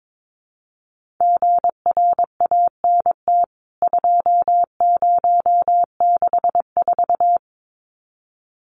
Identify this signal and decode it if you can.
Morse code, heard as a keyed tone, about 22 wpm, 700 Hz. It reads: GRANT 2064
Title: CWT Exchange 22wpm